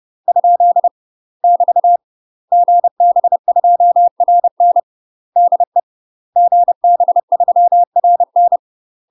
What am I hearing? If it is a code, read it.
Morse code, heard as a keyed tone, about 30 wpm, 700 Hz. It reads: ? <BT> GB2RN DE GB3RN
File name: UK MASTER QSOs TRAINING / 30WPM / UK MASTER QSOs 0018